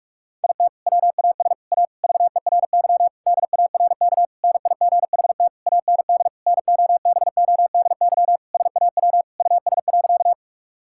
Morse code, heard as a keyed tone, about 46 wpm, 700 Hz. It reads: IT WAS A VERY DARK NIGHT AND NOBODY SAW US.
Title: Sentences from Top 1600 Words 45wpm